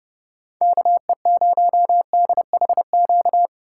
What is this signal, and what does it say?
Morse code, heard as a keyed tone, about 32 wpm, 700 Hz. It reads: KE0DHQ
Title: USA Call Signs - 6 Characters - 8x Character Spacing - 30wpm